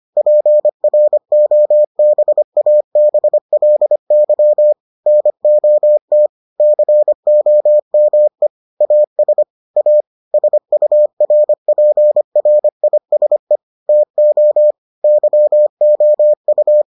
Morse code, heard as a keyed tone, about 25 wpm, 600 Hz. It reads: PROBABLY NOT COME AS A SURPRISE TO YOU